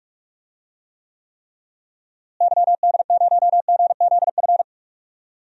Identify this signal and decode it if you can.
Morse code, heard as a keyed tone, about 42 wpm, 700 Hz. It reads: YD0GZF
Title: DXCC-100 Call Signs - 6 Characters - Repeated 3x - 45wpm